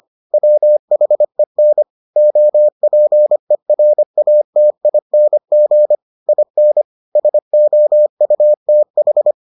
Morse code, heard as a keyed tone, about 25 wpm, 600 Hz. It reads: WHEN OPERATING IN SOUTH